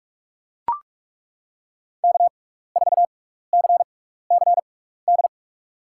Morse code, heard as a keyed tone, about 41 wpm, spaced as for 15 wpm, 700 Hz. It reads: K4CCD
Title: USA Call Signs - 5 Characters - 6x Character Spacing - 45wpm